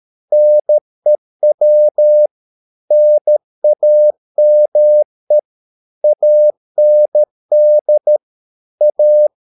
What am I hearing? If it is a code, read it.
Morse code, heard as a keyed tone, about 13 wpm, 600 Hz. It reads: NEW NAME AND A